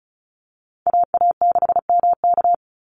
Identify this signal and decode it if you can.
Morse code, heard as a keyed tone, about 34 wpm, 700 Hz. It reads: AA6MK
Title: Call Signs Repeated 2x - 35wpm